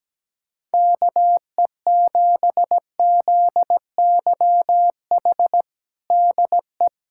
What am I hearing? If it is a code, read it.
Morse code, heard as a keyed tone, about 17 wpm, 700 Hz. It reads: KE7ZYH DE
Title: USA Callsign DE Callsign - 6 Characters - 17wpm